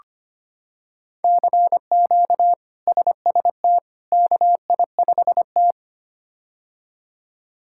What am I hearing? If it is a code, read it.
Morse code, heard as a keyed tone, about 25 wpm, 700 Hz. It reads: CQ SST KI5T
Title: K1USN SST QSO Practice - 25wpm